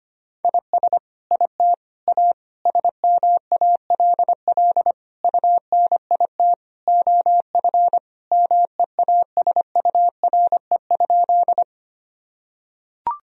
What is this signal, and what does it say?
Morse code, heard as a keyed tone, about 25 wpm, 700 Hz. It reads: IS IT A SMALL UNIT OF MEASURE?
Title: Sentences from Top 1000 Words - Review Part 2 - 25wpm